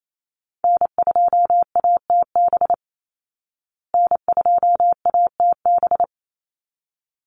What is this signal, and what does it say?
Morse code, heard as a keyed tone, about 28 wpm, 700 Hz. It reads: N2ATB N2ATB
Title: Call Signs Repeated 3x - 28wpm